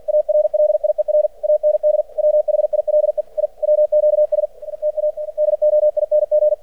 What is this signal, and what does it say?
Morse code, heard as a keyed tone, about 49 wpm, 600 Hz. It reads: ARLIER AND WHILE I WQS TA DOING
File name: archived / CRSnet